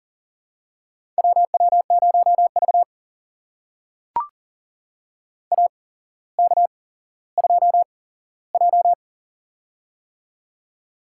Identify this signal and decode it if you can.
Morse code, heard as a keyed tone, about 38 wpm, spaced as for 17 wpm, 700 Hz. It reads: WW0V AK2J
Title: USA Call Signs - 4 Characters - 8x Character Spacing - 40wpm